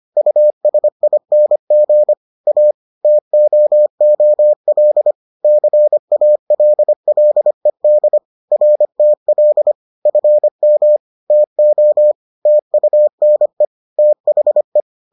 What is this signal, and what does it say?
Morse code, heard as a keyed tone, about 25 wpm, 600 Hz. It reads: USING A TOOL CALLED RTL FM TO TUNE THE